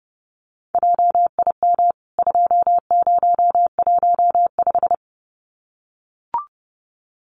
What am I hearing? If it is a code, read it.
Morse code, heard as a keyed tone, about 30 wpm, 700 Hz. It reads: JIM 2015
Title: CWT Exchange 30wpm